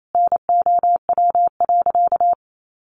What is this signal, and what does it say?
Morse code, heard as a keyed tone, about 28 wpm, 700 Hz. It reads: NOW.
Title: Sentences from Top 1100 Words 28wpm